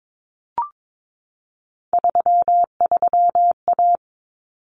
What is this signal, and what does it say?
Morse code, heard as a keyed tone, about 22 wpm, 700 Hz. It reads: 33A
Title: Common Cut Numbers - 3-Digit 22wpm